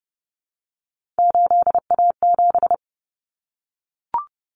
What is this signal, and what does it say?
Morse code, heard as a keyed tone, about 30 wpm, 700 Hz. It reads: 8A7
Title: Common Cut Numbers - 3-Digit 30wpm